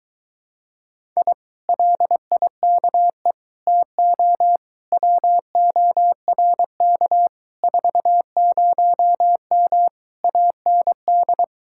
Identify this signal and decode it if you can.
Morse code, heard as a keyed tone, about 22 wpm, 700 Hz. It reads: I LIKE TO WORK 40M AND